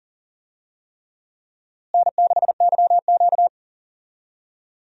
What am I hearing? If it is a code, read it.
Morse code, heard as a keyed tone, about 40 wpm, 700 Hz. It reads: N6YQ
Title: USA Call Signs - 4 Characters - Repeated 3x - 40wpm